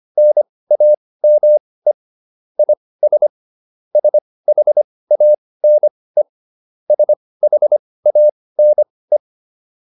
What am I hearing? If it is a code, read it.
Morse code, heard as a keyed tone, about 25 wpm, spaced as for 18 wpm, 600 Hz. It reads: NAME IS SHANE SHANE
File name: QSO106 18